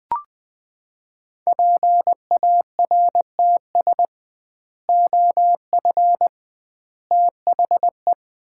Morse code, heard as a keyed tone, about 20 wpm, 700 Hz. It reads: PARTS OF THE
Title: 2x Word-Spacing - Sets of 3 Words - 20wpm